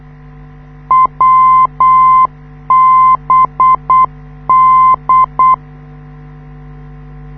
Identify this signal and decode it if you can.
Morse code, heard as a keyed tone, about 8 wpm, 1040 Hz. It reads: WBD
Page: this is how the NDB of the Wiesbaden US Army Airfield at Erbenheim sounds like, when you live 2 km away